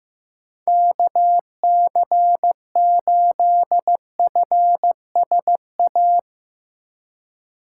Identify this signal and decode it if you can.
Morse code, heard as a keyed tone, about 15 wpm, 700 Hz. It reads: KC8FSA
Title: USA Call Signs - 4x Character Spacing - 15wpm